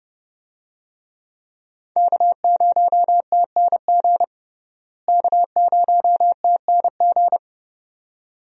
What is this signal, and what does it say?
Morse code, heard as a keyed tone, about 30 wpm, 700 Hz. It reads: K0TNG K0TNG
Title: USA Call Signs - 5 Characters - Repeated 2x - 30wpm